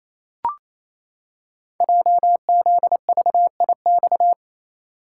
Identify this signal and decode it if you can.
Morse code, heard as a keyed tone, about 28 wpm, 700 Hz. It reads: JZVIX
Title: Code Groups - 5 Letters - 28wpm